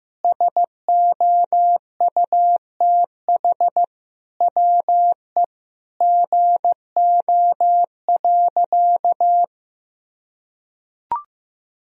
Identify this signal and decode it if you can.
Morse code, heard as a keyed tone, about 15 wpm, 700 Hz. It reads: SOUTH WE GO.
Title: Sentences from Top 1000 Words - Review Part 2 - 15wpm